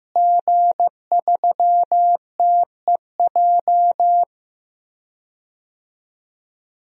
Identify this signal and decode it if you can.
Morse code, heard as a keyed tone, about 15 wpm, 700 Hz. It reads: G3TEJ
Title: DXCC-100 Call Signs - 5 Characters - Repeated 3x - 15wpm